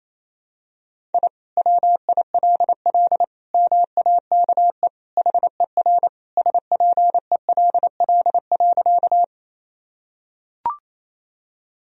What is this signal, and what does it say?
Morse code, heard as a keyed tone, about 28 wpm, 700 Hz. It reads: I WILL MAKE HER SPELL.
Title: Sentences from Top 1000 Words - Review Part 1 - 28wpm